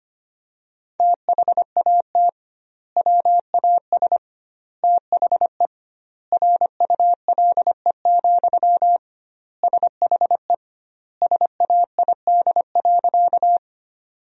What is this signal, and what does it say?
Morse code, heard as a keyed tone, about 25 wpm, 700 Hz. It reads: THAT WAS THE RULE, SHE SAID.
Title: 2x Word-Spacing - Sentences from Top 400 Words - 25wpm